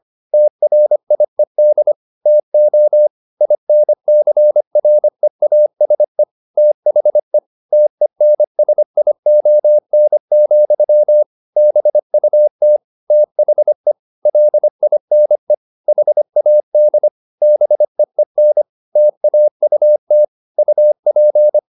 Morse code, heard as a keyed tone, about 25 wpm, 600 Hz. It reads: TRIED TO INCREASE THE TENSION, BUT THE LINE HAD BEEN TAUT UP
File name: the old man and the sea part 1 25wpm